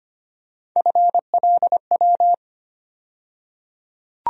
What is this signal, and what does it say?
Morse code, heard as a keyed tone, about 25 wpm, 700 Hz. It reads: FLW
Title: USA Call Sign Postfixes - 3 Letters 25wpm